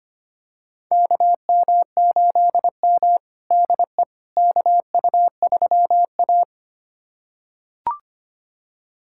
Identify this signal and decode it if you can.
Morse code, heard as a keyed tone, about 25 wpm, 700 Hz. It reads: KM8M DE KU3A
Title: USA Callsign DE Callsign - 4 Characters - 25wpm